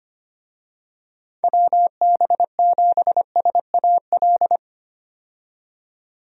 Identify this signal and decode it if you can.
Morse code, heard as a keyed tone, about 25 wpm, 700 Hz. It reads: WB7SAL
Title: USA Call Signs - 6 Characters - 25wpm